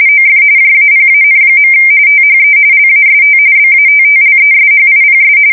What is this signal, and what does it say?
RTTY, about 45.45 baud. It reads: OF JAPAN. THE LAST BATTLESHIPS
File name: fsk170 / USS-Missouri